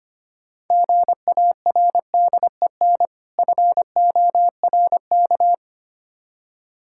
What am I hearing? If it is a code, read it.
Morse code, heard as a keyed tone, about 27 wpm, 700 Hz. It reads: GARDEN FORK
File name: Difficult Words - Speed-Racing with Hints - 25wpm